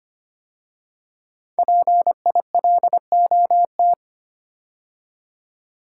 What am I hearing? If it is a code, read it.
Morse code, heard as a keyed tone, about 25 wpm, 700 Hz. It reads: PILOT
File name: 5 Letter Words - First 3 Letters Given - 25wpm